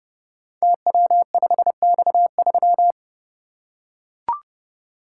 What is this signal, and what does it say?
Morse code, heard as a keyed tone, about 30 wpm, 700 Hz. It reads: TW5X3
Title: Code Groups - 5 Letter:Numbers - 30wpm